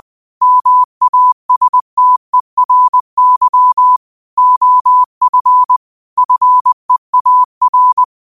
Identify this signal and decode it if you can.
Morse code, heard as a keyed tone, about 20 wpm, 1000 Hz. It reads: MASTERY OF FEAR
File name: qotdfiles / 20wpm